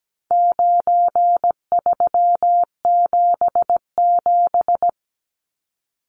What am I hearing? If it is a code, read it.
Morse code, heard as a keyed tone, about 17 wpm, 700 Hz. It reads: 9377
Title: Numbers - 4-Digit 17wpm